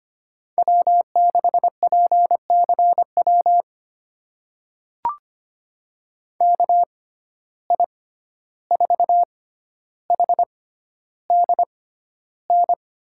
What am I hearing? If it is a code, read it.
Morse code, heard as a keyed tone, about 25 wpm, spaced as for 12 wpm, 700 Hz. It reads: W6PCW KI4HDN
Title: USA Call Signs - 6x Character Spacing - 25wpm